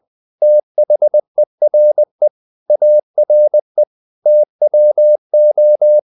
Morse code, heard as a keyed tone, about 20 wpm, 600 Hz. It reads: THERE ARE TWO
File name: the old man and the sea / the old man and the sea part 1 20wpm